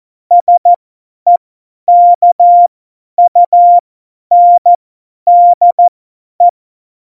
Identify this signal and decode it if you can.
Morse code, heard as a keyed tone, about 14 wpm, spaced as for 10 wpm, 700 Hz. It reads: SEKUNDE
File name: Morsekurs / Witze / Witz 29. September 2020, 50 F70